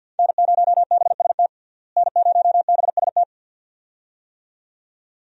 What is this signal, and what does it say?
Morse code, heard as a keyed tone, about 51 wpm, 700 Hz. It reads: N0BST N0BST
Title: USA Call Signs - 5 Characters - Repeated 2x - 50wpm